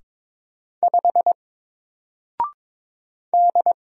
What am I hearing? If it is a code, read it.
Morse code, heard as a keyed tone, about 22 wpm, 700 Hz. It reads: 5 D
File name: Single-Letter-Number-22wpm